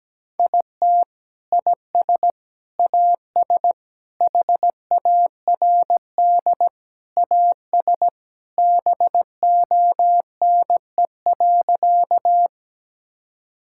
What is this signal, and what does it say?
Morse code, heard as a keyed tone, about 17 wpm, 700 Hz. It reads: IT IS AS HARD AS BONE.
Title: Sentences from Top 1600 Words 17wpm